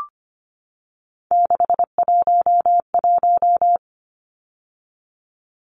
Morse code, heard as a keyed tone, about 25 wpm, 700 Hz. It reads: 611
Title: Numbers - 3-Digit 25wpm